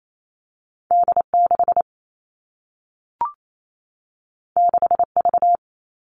Morse code, heard as a keyed tone, about 28 wpm, 700 Hz. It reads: D6 6V
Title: All Cut Numbers - 2-Digit 28wpm